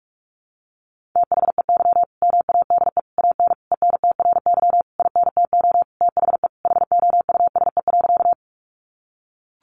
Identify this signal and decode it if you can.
Morse code, heard as a keyed tone, about 44 wpm, 700 Hz. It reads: THEY MADE AN ENTRY INTO THE HOUSE.
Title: Sentences from Top 1500 Words 45wpm